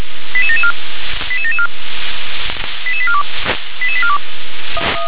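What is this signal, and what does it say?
Five-tone selective call (ZVEI), heard as quick runs of five tones, tones of about 70 ms. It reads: 88774 88774 88842 88842
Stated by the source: ZVEI five-tone selective call received by SDR.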